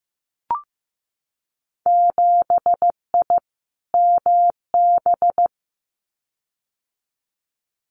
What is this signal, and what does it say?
Morse code, heard as a keyed tone, about 15 wpm, 700 Hz. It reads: 7I MB
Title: Winter Field Day Exchanges Encore 15wpm